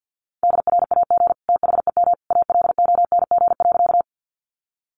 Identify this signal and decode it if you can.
Morse code, heard as a keyed tone, about 50 wpm, 700 Hz. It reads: DRAG THEM ALONG.
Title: Sentences from Top 1500 Words 50wpm